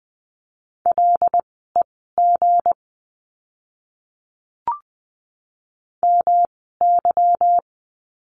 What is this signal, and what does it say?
Morse code, heard as a keyed tone, about 20 wpm, spaced as for 15 wpm, 700 Hz. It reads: LEG MY
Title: 160 Lesson 26f - Letter g - Words 20wpm